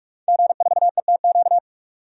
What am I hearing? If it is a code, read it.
Morse code, heard as a keyed tone, about 46 wpm, 700 Hz. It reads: G4ETQ
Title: DXCC-100 Call Signs - 5 Characters - 45wpm